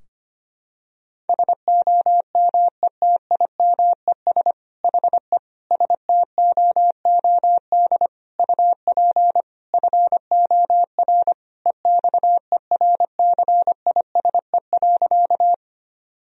Morse code, heard as a keyed tone, about 25 wpm, 700 Hz. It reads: SOMETIMES HE STOOD UP FOR EXERCISE.
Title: Sentences from Top 1000 Words - Review Part 4 - 25wpm